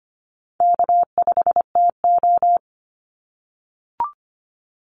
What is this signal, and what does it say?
Morse code, heard as a keyed tone, about 25 wpm, 700 Hz. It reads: K5TO
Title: Easy Call Signs - 25wpm